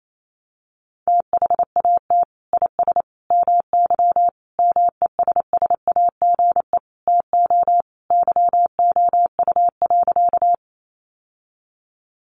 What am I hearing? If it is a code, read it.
Morse code, heard as a keyed tone, about 28 wpm, 700 Hz. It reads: THAT IS MY MESSAGE TO YOU.
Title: Sentences from Top 1100 Words 28wpm